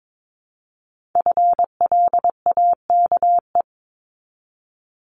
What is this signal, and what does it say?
Morse code, heard as a keyed tone, about 22 wpm, 700 Hz. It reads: FLAKE
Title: Multi-letter Phonemes 22wpm